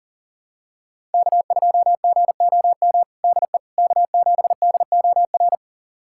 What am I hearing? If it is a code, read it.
Morse code, heard as a keyed tone, about 40 wpm, 700 Hz. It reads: K2GOM DE K7DOR
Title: USA Callsign DE Callsign - 5 Characters - 40wpm